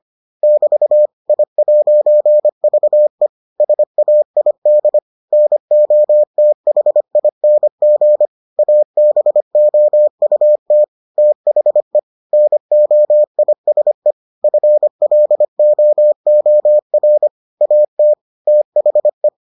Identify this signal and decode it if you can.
Morse code, heard as a keyed tone, about 25 wpm, 600 Hz. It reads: <BT> I'VE SAID NOTHING ABOUT THE NOISE FLOOR AT THE